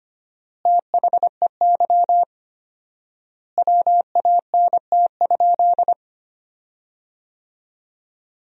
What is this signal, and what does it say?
Morse code, heard as a keyed tone, about 25 wpm, 700 Hz. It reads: THEY WANT?
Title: 4x Word-Spacing - Sentences from Top 200 Words - 25wpm